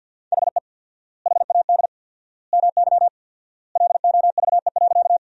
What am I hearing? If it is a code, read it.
Morse code, heard as a keyed tone, about 47 wpm, 700 Hz. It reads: HE HAD MY LOVE.
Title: 4x Word-Spacing - Sentences from Top 400 Words - 50wpm